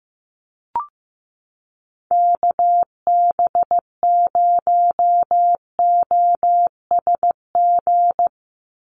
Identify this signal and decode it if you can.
Morse code, heard as a keyed tone, about 15 wpm, 700 Hz. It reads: KB0OSG
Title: USA Call Signs 15wpm